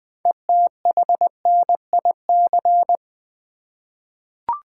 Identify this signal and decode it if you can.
Morse code, heard as a keyed tone, about 20 wpm, 700 Hz. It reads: ETHNIC
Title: 6 Letter Words - First 3 Letters Given - 20wpm